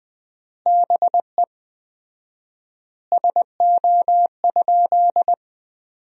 Morse code, heard as a keyed tone, about 20 wpm, 700 Hz. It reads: BE SO?
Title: 4x Word-Spacing - Sentences from Top 300 Words - 20wpm